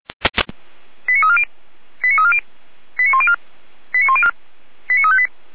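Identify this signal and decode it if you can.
Five-tone selective call (ZVEI), heard as quick runs of five tones, tones of about 70 ms. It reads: 89340 89340 89104 89358
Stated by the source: ZVEI five-tone selective call received on an SDR.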